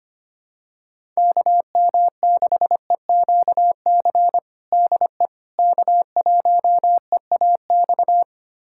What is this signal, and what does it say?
Morse code, heard as a keyed tone, about 25 wpm, 700 Hz. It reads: KM6EQC DE K1EAX
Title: USA Callsign DE Callsign 25wpm